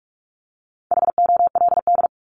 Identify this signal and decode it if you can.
Morse code, heard as a keyed tone, about 44 wpm, 700 Hz. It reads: HOLD
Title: ING Suffix - Top 100 Words with Base Words 45wpm